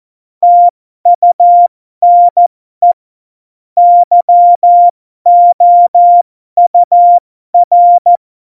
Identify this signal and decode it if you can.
Morse code, heard as a keyed tone, about 14 wpm, spaced as for 12 wpm, 700 Hz. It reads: TUNE YOUR
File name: Morsekurs / Witze